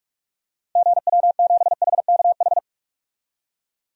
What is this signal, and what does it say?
Morse code, heard as a keyed tone, about 43 wpm, 700 Hz. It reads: GW7HKH
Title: DXCC-100 Call Signs - 6 Characters - 45wpm